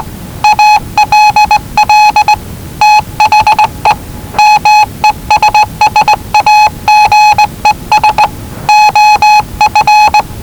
Morse code, heard as a keyed tone, about 19 wpm, 850 Hz. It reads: ALL THE MESSAGES OF
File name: WellingtonRadioZlwCloseDownOnHfAndMf / 01-Zlw 4277khz 417.5 qrt until 0925z